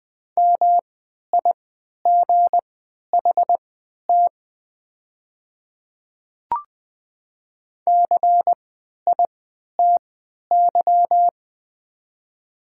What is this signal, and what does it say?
Morse code, heard as a keyed tone, about 20 wpm, spaced as for 11 wpm, 700 Hz. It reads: MIGHT CITY
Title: 3x Character Spacing - Top 300 Words - 20wpm